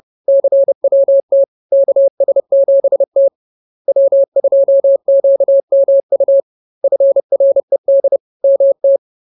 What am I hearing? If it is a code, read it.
Morse code, heard as a keyed tone, about 30 wpm, 550 Hz. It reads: CWT KS7T W2QMU FRED MT